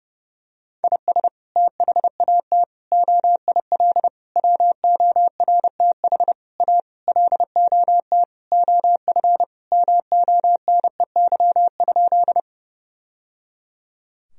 Morse code, heard as a keyed tone, about 29 wpm, 700 Hz. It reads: IS THAT OIL WORTH A LOT OF MONEY?
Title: Sentences from Top 1000 Words - Review Part 4 - 30wpm